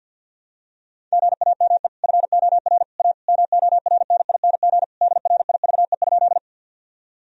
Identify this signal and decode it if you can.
Morse code, heard as a keyed tone, about 49 wpm, 700 Hz. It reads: GAME FOR A MORNING DRIVE?
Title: Sentences from Top 1000 Words - Review Part 3 - 50wpm